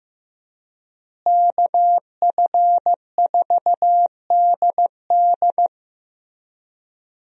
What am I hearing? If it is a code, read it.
Morse code, heard as a keyed tone, about 15 wpm, 700 Hz. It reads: KF4DD
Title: USA Call Signs - 5 Characters - Repeated 3x - 15wpm